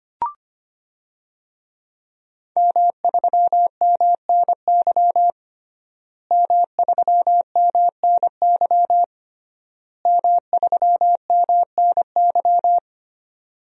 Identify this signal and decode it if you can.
Morse code, heard as a keyed tone, about 25 wpm, 700 Hz. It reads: M3MNY M3MNY M3MNY
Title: DXCC-100 Call Signs - 5 Characters - Repeated 3x - 25wpm